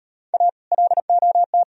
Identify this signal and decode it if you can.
Morse code, heard as a keyed tone, about 38 wpm, 700 Hz. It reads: A LOT